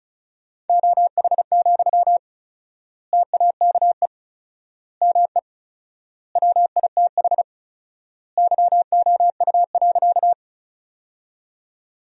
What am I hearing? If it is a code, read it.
Morse code, heard as a keyed tone, about 36 wpm, 700 Hz. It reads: OH, TAKE ME WITH YOU.
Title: 4x Word-Spacing - Sentences from Top 500 Words - 35wpm